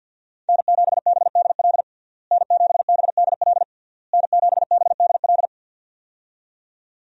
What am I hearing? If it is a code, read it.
Morse code, heard as a keyed tone, about 52 wpm, 700 Hz. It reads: N7BDL N7BDL N7BDL
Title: USA Call Signs - 5 Characters - Repeated 3x - 50wpm